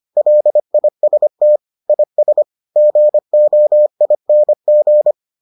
Morse code, heard as a keyed tone, about 25 wpm, 600 Hz. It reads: LIST IS GOING